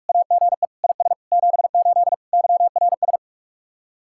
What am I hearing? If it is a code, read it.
Morse code, heard as a keyed tone, about 45 wpm, 700 Hz. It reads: AGE IS 78 YRS